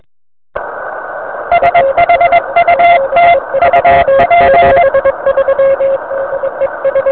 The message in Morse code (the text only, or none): SHUT UP